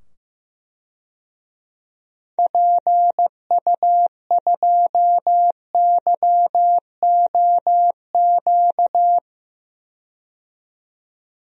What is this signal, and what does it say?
Morse code, heard as a keyed tone, about 15 wpm, 700 Hz. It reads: PU2YOQ
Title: DXCC-100 Call Signs - 6 Characters - 15wpm